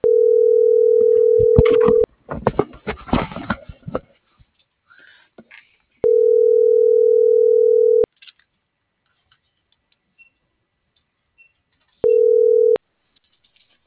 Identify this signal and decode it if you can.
Ringback tone (North America), 440 + 480 Hz.